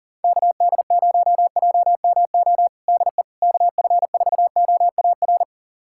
Morse code, heard as a keyed tone, about 40 wpm, 700 Hz. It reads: KD0JMO DE KF4OAR
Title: USA Callsign DE Callsign - 6 Characters - 40wpm